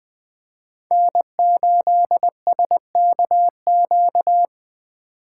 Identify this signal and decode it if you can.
Morse code, heard as a keyed tone, about 20 wpm, 700 Hz. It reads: N8SKQ
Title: USA Call Signs - 5 Characters - 3x Character Spacing - 20wpm